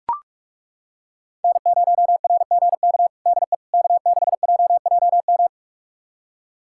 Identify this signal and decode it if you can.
Morse code, heard as a keyed tone, about 41 wpm, 700 Hz. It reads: N0RGK DE K6JJM
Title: USA Callsign DE Callsign - 5 Characters - 45wpm